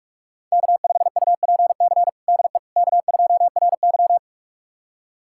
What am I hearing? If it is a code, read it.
Morse code, heard as a keyed tone, about 46 wpm, 700 Hz. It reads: K5UPC DE K2RY
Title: USA Callsign DE Callsign 45wpm